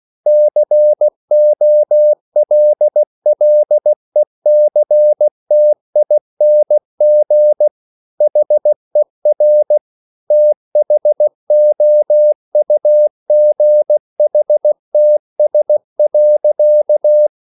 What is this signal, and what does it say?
Morse code, heard as a keyed tone, about 16 wpm, 600 Hz. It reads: COLLECTING HER THOUGHTS.